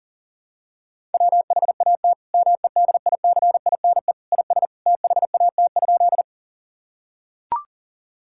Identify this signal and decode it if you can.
Morse code, heard as a keyed tone, about 37 wpm, 700 Hz. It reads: WHAT MEDICINE IS THAT?
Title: Sentences from Top 1000 Words - Review Part 6 - 40wpm